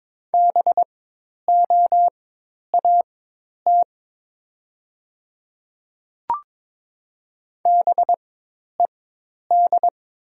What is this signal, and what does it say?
Morse code, heard as a keyed tone, about 23 wpm, spaced as for 10 wpm, 700 Hz. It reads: BOAT BED
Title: 4x Character Spacing - Top 500 Words - 22wpm